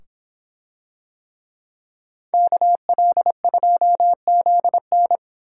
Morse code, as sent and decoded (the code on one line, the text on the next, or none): -.- .-.. ..--- --.. -.
KL2ZN